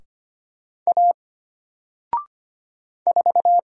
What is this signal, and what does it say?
Morse code, heard as a keyed tone, about 25 wpm, 700 Hz. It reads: A 4
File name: Single-Letter-Number-25wpm